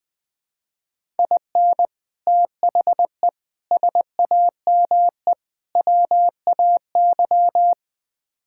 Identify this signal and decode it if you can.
Morse code, heard as a keyed tone, about 21 wpm, 700 Hz. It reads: IN THE SAME WAY
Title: Build A Phrase - 4-Word Phrases - 20wpm